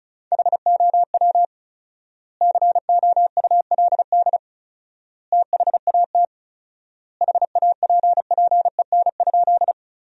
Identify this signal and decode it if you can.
Morse code, heard as a keyed tone, about 36 wpm, 700 Hz. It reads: HOW COULD THAT HAPPEN?
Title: 4x Word-Spacing - Sentences from Top 400 Words - 35wpm